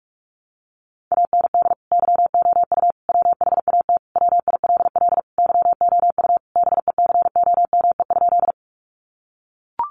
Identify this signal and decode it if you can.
Morse code, heard as a keyed tone, about 44 wpm, 700 Hz. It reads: AND YOU WHAT WILL YOU BECOME?
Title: Sentences from Top 600 Words 45wpm